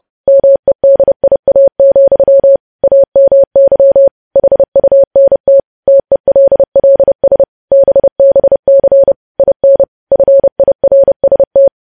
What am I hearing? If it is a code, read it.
Morse code, heard as a keyed tone, about 30 wpm, 550 Hz. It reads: MEDIA, AMY HUNT TELLS BBC IN FIRST
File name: podcast / News headlines-1x-30x30